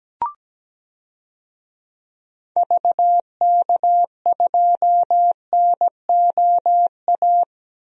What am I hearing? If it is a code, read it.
Morse code, heard as a keyed tone, about 17 wpm, 700 Hz. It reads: VK2NOA
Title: DXCC-100 Call Signs - 6 Characters - Repeated 2x - 17wpm